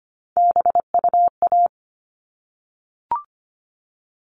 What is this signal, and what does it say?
Morse code, heard as a keyed tone, about 25 wpm, 700 Hz. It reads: BUA
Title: All Cut Numbers - 3-Digit 25wpm